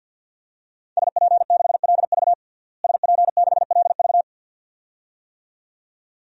Morse code, heard as a keyed tone, about 51 wpm, 700 Hz. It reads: SP6LV SP6LV
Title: DXCC-100 Call Signs - 5 Characters - Repeated 2x - 50wpm